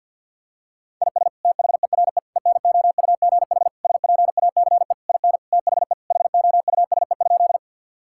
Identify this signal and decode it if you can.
Morse code, heard as a keyed tone, about 52 wpm, 700 Hz. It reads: IS THERE ENOUGH SPACE IN THE HOUSE?
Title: Sentences from Top 1000 Words - Review Part 5 - 50wpm